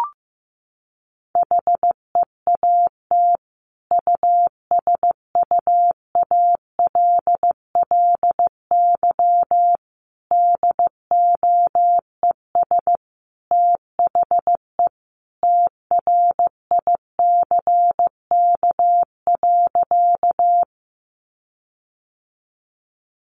Morse code, heard as a keyed tone, about 15 wpm, 700 Hz. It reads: HEAT USUALLY DOES THE TRICK.
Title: Sentences from Top 1500 Words 15wpm